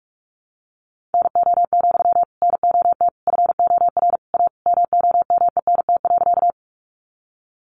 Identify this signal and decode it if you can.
Morse code, heard as a keyed tone, about 44 wpm, 700 Hz. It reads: NO, NOT FOR A MOMENT.
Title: Sentences from Top 1000 Words 45wpm